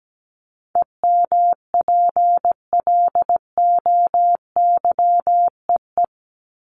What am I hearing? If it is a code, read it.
Morse code, heard as a keyed tone, about 17 wpm, 700 Hz. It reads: EMPLOYEE
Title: Top 1501-1750 Words 17wpm